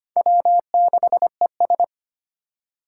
Morse code, heard as a keyed tone, about 25 wpm, 700 Hz. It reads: W6ES
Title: Four Challenging Letter-Number Combinations 25wpm